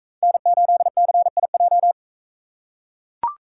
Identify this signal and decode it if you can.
Morse code, heard as a keyed tone, about 43 wpm, 700 Hz. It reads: N8CIJ